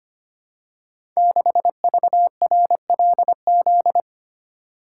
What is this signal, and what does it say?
Morse code, heard as a keyed tone, about 25 wpm, 700 Hz. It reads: 6VRLZ
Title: Code Groups - 5 Letter:Numbers - 25wpm